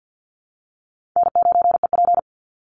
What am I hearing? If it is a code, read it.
Morse code, heard as a keyed tone, about 50 wpm, 700 Hz. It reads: N9EP
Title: Easy Call Signs Repeated 2x - 50wpm